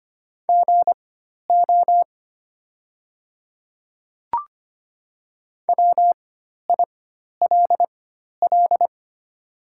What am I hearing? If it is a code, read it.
Morse code, heard as a keyed tone, about 25 wpm, spaced as for 12 wpm, 700 Hz. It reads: GO WILL
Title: 4x Character Spacing - Top 100 Words - 25wpm